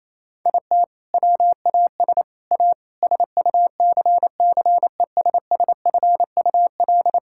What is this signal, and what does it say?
Morse code, heard as a keyed tone, about 28 wpm, 700 Hz. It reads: IT WAS A SUCCESSFUL
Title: Sentences from Top 1000 Words - Review Part 6 - 28wpm